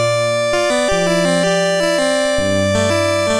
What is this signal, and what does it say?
There is no code.